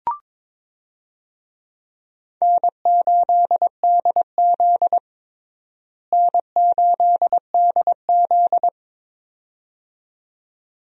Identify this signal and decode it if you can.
Morse code, heard as a keyed tone, about 22 wpm, 700 Hz. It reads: N8DZ N8DZ
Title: USA Call Signs - 4 Characters - Repeated 2x - 22wpm